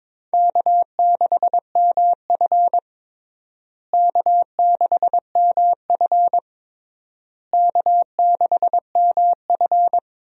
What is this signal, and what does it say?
Morse code, heard as a keyed tone, about 21 wpm, 700 Hz. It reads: K6MF K6MF K6MF
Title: USA Call Signs - 4 Characters - Repeated 3x - 22wpm